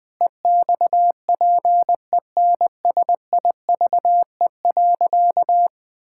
Morse code, heard as a keyed tone, about 21 wpm, 700 Hz. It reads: EXPENSIVE.
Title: Pangram Challenge v2 - 20wpm